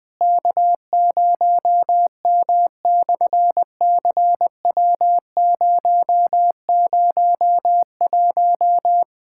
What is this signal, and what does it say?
Morse code, heard as a keyed tone, about 20 wpm, 700 Hz. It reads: K0M/CW001
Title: SOTA QSO Practice 20wpm